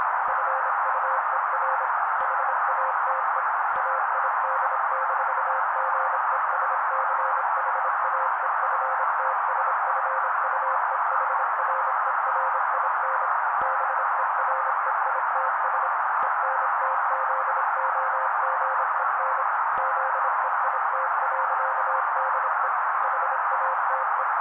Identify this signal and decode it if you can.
Morse code, heard as a keyed tone, about 26 wpm, 530 Hz. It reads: FUER HATE AID-GESCHAEFTSFUEHRERIN BEREITS ENTZOGEN ZEIT.DE HATE
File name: daily / 25wpm noise